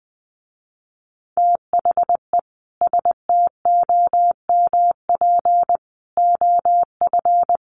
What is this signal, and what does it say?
Morse code, heard as a keyed tone, about 20 wpm, 700 Hz. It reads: THE STOMP OF
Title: A Sound Of Thunder 20wpm